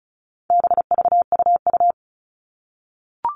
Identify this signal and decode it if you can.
Morse code, heard as a keyed tone, about 34 wpm, 700 Hz. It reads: BVUU
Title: All Cut Numbers - 4-Digit 35wpm